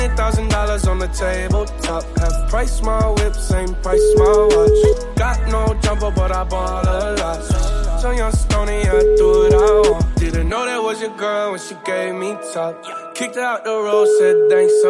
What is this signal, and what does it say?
Ringback tone (Europe, 425 Hz), with speech.